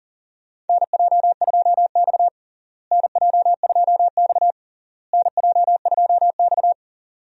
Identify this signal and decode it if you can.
Morse code, heard as a keyed tone, about 40 wpm, 700 Hz. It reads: NJ2X NJ2X NJ2X
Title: USA Call Signs - 4 Characters - Repeated 3x - 40wpm